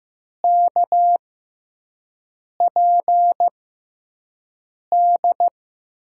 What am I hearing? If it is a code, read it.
Morse code, heard as a keyed tone, about 15 wpm, spaced as for 5 wpm, 700 Hz. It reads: KPD
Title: USA Call Signs - 6x Character Spacing - 15wpm